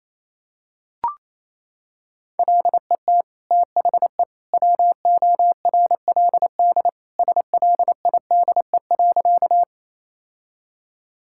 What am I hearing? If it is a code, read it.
Morse code, heard as a keyed tone, about 28 wpm, 700 Hz. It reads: LET THE WORLD SLIDE.
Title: Sentences from Top 1600 Words 28wpm